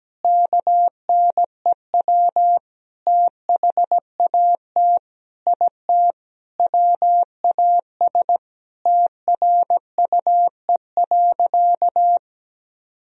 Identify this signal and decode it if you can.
Morse code, heard as a keyed tone, about 17 wpm, 700 Hz. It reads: KNEW THAT IT WAS TRUE.
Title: Sentences from Top 400 Words and Abbreviations 17wpm